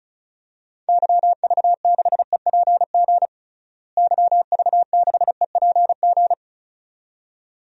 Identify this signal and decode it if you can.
Morse code, heard as a keyed tone, about 36 wpm, 700 Hz. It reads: YV6EPG YV6EPG
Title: DXCC-100 Call Signs - 6 Characters - Repeated 2x - 35wpm